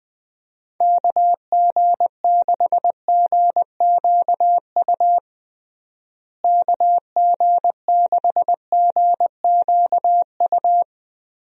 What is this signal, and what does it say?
Morse code, heard as a keyed tone, about 19 wpm, 700 Hz. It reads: KG6GQU KG6GQU
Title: USA Call Signs - 6 Characters - Repeated 3x - 20wpm